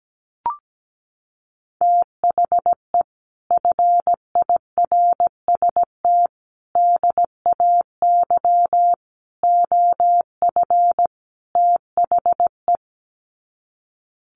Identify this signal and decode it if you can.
Morse code, heard as a keyed tone, about 17 wpm, 700 Hz. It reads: THE FIRST DAY OF THE
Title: Sets of 5 Words Encore2 17wpm